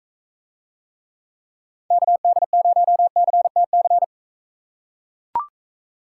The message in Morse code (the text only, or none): KD0CTC